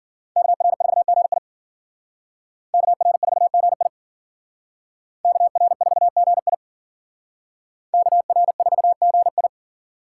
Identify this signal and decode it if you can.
Morse code, heard as a keyed tone, about 51 wpm, 700 Hz. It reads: KR4GI KR4GI KR4GI KR4GI